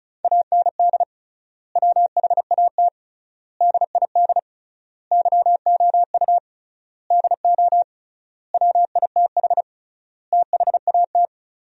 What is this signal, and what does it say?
Morse code, heard as a keyed tone, about 36 wpm, 700 Hz. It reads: AND WHAT DID YOU DO WITH THAT
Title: 3x Word-Spacing - Sentences from Top 300 Words - 35wpm